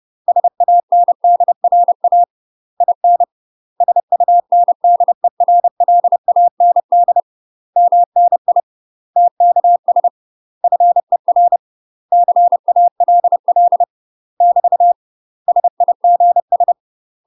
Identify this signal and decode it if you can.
Morse code, heard as a keyed tone, about 30 wpm, 700 Hz. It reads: SANDRA IN SUNDERLAND MNI TKS FER CALL <BT> SIGS
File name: UK MASTER QSOs TRAINING / 30WPM / UK MASTER QSOs 0030